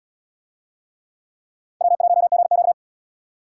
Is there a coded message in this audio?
Yes: Morse code from 1.8 to 2.7 s, heard as a keyed tone, about 76 wpm, 700 Hz. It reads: W1GQ